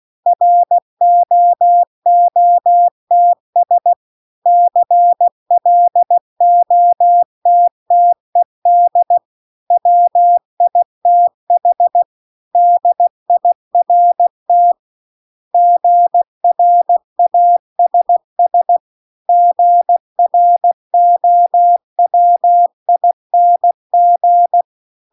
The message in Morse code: ROOTS CLOTTED WITH DIRT GRASS GROWING

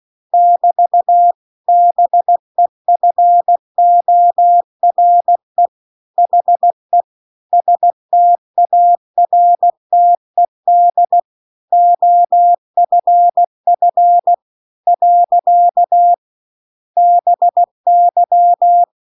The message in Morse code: <BT> BEFORE HE STARTED OFF . BY

